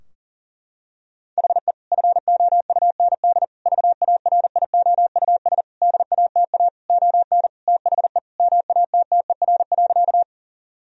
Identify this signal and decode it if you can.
Morse code, heard as a keyed tone, about 39 wpm, 700 Hz. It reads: HE FOUND VARIOUS DATA ON THE MATTER.
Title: Sentences from Top 1000 Words - Review Part 4 - 40wpm